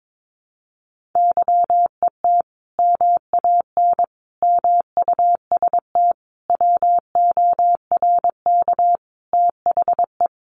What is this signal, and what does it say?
Morse code, heard as a keyed tone, about 22 wpm, 700 Hz. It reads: YET MAN MUST WORK THE
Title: Sentences from Top 1500 Words 22wpm